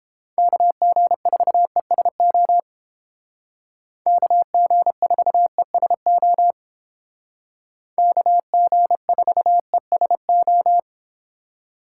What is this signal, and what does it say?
Morse code, heard as a keyed tone, about 29 wpm, 700 Hz. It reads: KG4ESO KG4ESO KG4ESO